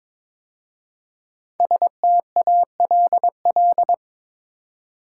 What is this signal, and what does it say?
Morse code, heard as a keyed tone, about 22 wpm, 700 Hz. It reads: STALL
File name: 5 Letter Words - First 3 Letters Given - 22wpm